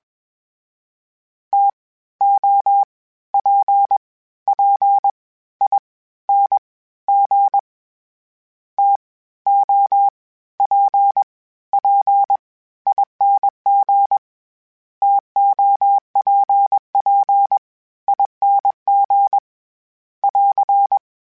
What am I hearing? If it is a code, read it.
Morse code, heard as a keyed tone, about 21 wpm, spaced as for 14 wpm, 800 Hz. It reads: TOPPING TOPPING TOPPING <AR>
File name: Misc / Misc-mp3s / TO @20